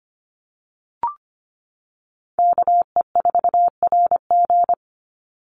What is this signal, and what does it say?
Morse code, heard as a keyed tone, about 25 wpm, 700 Hz. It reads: KE4RG
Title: Call Signs - 25wpm